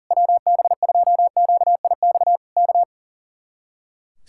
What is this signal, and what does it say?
Morse code, heard as a keyed tone, about 40 wpm, 700 Hz. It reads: WB2QIX K
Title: Speed-Racing Common QSO Phrases - 40wpm